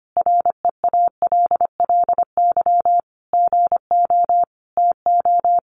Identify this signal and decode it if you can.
Morse code, heard as a keyed tone, about 25 wpm, 700 Hz. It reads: REALLY GO TO
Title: Sentences from Top 600 Words 25wpm